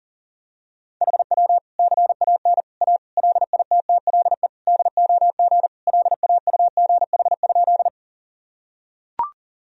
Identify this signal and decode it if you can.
Morse code, heard as a keyed tone, about 40 wpm, 700 Hz. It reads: HW CAN A LITTLE DOG LAUGH?
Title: Sentences from Top 500 Words and Abbreviations 40wpm